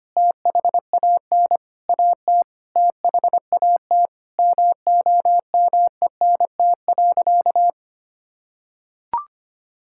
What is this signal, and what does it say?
Morse code, heard as a keyed tone, about 25 wpm, 700 Hz. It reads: THAN AT THAT MOMENT.
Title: Sentences from Top 1000 Words - Review Part 6 - 25wpm